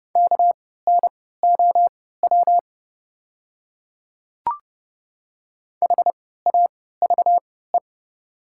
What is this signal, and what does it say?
Morse code, heard as a keyed tone, about 30 wpm, spaced as for 17 wpm, 700 Hz. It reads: KNOW HAVE
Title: 3x Character Spacing - Top 100 Words - 30wpm